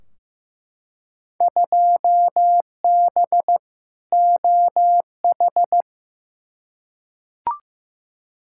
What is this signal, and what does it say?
Morse code, heard as a keyed tone, about 15 wpm, 700 Hz. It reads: 2B OH
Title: Field Day Exchanges Encore - 15wpm